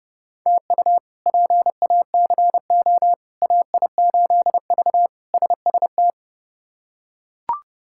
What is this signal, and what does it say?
Morse code, heard as a keyed tone, about 32 wpm, 700 Hz. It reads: TU PACO AI8V SST
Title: K1USN SST QSO Practice - 30wpm